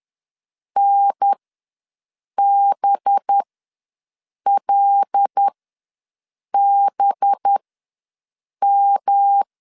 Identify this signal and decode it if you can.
Morse code, heard as a keyed tone, about 10 wpm, spaced as for 6 wpm, 800 Hz. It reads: NBLBM
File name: LEZIONI / 5 LBVC / 5 LB ws 10wpm 3616